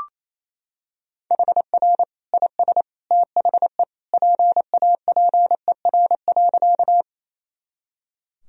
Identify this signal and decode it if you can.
Morse code, heard as a keyed tone, about 28 wpm, 700 Hz. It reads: HR IS THE PAPER.
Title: Sentences from Top 300 Words and Abbreviations 28wpm